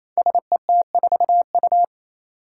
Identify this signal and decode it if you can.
Morse code, heard as a keyed tone, about 27 wpm, 700 Hz. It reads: SET4U
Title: Five Challenging Letter-Number Combinations 28wpm